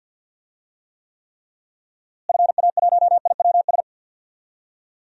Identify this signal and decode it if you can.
Morse code, heard as a keyed tone, about 50 wpm, 700 Hz. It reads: FA1IWS
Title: DXCC-100 Call Signs - 6 Characters - Repeated 3x - 50wpm